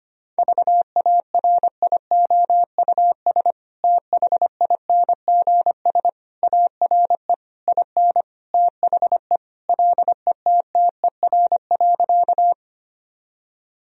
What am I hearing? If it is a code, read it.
Morse code, heard as a keyed tone, about 25 wpm, 690 Hz. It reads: VARIOUS THINGS ARE IN THE LETTER.
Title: Sentences from Top 1000 Words - Review Part 4 - 25wpm